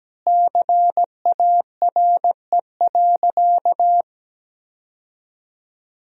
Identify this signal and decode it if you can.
Morse code, heard as a keyed tone, about 17 wpm, 700 Hz. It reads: CARE.
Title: 2x Word-Spacing - Sentences from Top 300 Words - 17wpm